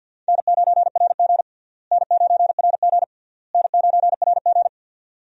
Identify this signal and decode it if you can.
Morse code, heard as a keyed tone, about 50 wpm, 700 Hz. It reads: N9RG N9RG N9RG
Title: USA Call Signs - 4 Characters - Repeated 3x - 50wpm